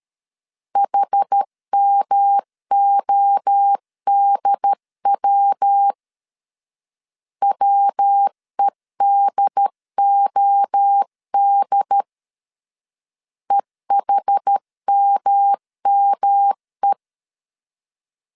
Morse code, heard as a keyed tone, about 13 wpm, 800 Hz. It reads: HMODW WEDOD EHMME
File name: CW / DET / 1 3 1500C S A 12wpm 2905